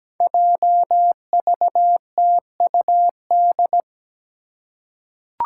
Morse code, heard as a keyed tone, about 17 wpm, 700 Hz. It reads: JVTUD
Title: Five Challenging Letter-Number Combinations 17wpm